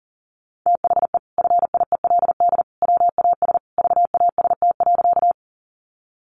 Morse code, heard as a keyed tone, about 40 wpm, 700 Hz. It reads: THE FIELD WAS VAST.
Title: Sentences from Top 1300 Words 40wpm